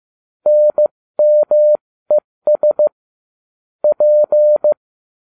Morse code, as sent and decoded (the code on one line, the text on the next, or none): -. -- . ... / .--.
NMES P